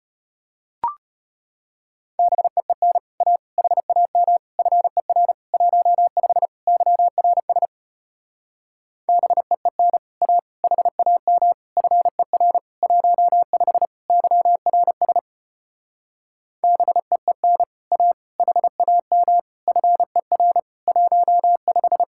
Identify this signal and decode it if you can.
Morse code, heard as a keyed tone, about 33 wpm, 700 Hz. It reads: BEEN A HAM FER 15 YRS BEEN A HAM FER 15 YRS BEEN A HAM FER 15